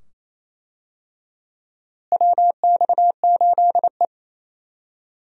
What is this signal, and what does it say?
Morse code, heard as a keyed tone, about 28 wpm, 700 Hz. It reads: WX8E
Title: USA Call Signs - 4 Characters - Repeated 2x - 28wpm